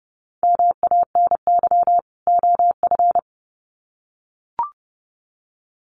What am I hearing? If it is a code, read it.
Morse code, heard as a keyed tone, about 30 wpm, 700 Hz. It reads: MANY OF
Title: Sets of 2 Words 30wpm